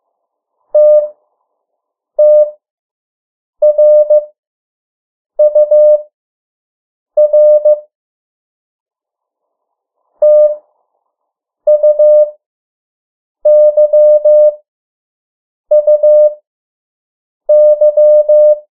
Morse code, heard as a keyed tone, about 15 wpm, spaced as for 6 wpm, 600 Hz. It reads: TTRUR TUYUY
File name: morse mp3 / 03 urok 2